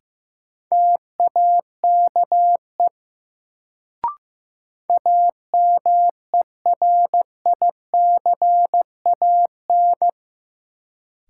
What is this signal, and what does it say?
Morse code, heard as a keyed tone, about 15 wpm, 700 Hz. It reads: TAKE AMERICAN
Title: Sets of 2 Words - New Word Drill - 15wpm